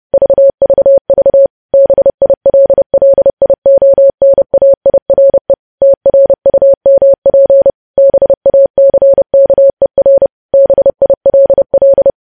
Morse code, heard as a keyed tone, about 30 wpm, 550 Hz. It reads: VVV BILLIONAIRE TRUMP BACKER BILL
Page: news headlines in morse code at 30 wpm